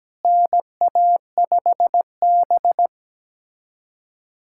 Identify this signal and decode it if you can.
Morse code, heard as a keyed tone, about 17 wpm, 700 Hz. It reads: NA5B
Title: USA Callsign DE Callsign - 4 Characters - 17wpm